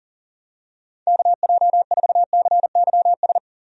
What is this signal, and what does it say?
Morse code, heard as a keyed tone, about 40 wpm, 700 Hz. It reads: KJ4CYS
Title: USA Call Signs - 6 Characters - 40wpm